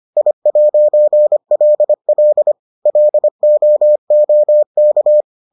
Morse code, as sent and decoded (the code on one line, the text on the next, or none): .. .----. .-.. .-.. / .-.. --- --- -.-
I'LL LOOK